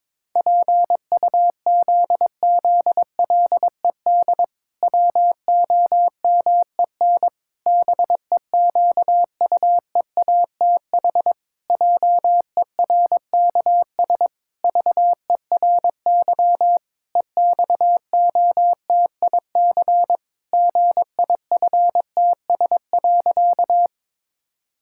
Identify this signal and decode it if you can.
Morse code, heard as a keyed tone, about 22 wpm, 700 Hz. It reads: PUZZLED WOMEN BEQUEATH JERKS VERY EXOTIC GIFTS.
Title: Pangram Challenge v2 - 22wpm